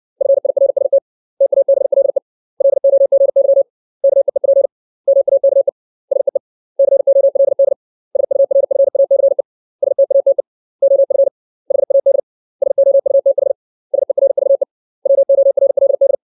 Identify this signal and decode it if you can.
Morse code, heard as a keyed tone, about 60 wpm, 550 Hz. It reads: FIRST TABLE LOGY KEEP MAKE SEE COLD SURFACE STATE OF HAD SOUTH SERVE WORLD
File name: 632 words / 632 Words 60WPM